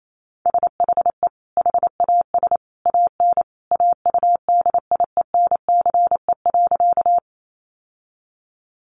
Morse code, heard as a keyed tone, about 28 wpm, 700 Hz. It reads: SHE HAS AN AUDIENCE.
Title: Sentences from Top 800 Words 28wpm